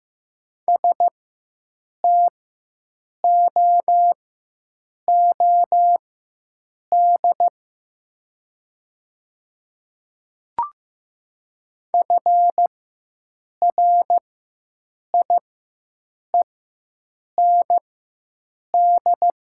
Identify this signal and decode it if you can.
Morse code, heard as a keyed tone, about 15 wpm, spaced as for 7 wpm, 700 Hz. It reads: STOOD FRIEND
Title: 4x Character Spacing - Top 500 Words - 15wpm